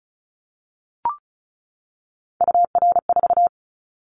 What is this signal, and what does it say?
Morse code, heard as a keyed tone, about 36 wpm, 700 Hz. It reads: UR4
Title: Callsign Prefixes for Top 25 DXCC - 35wpm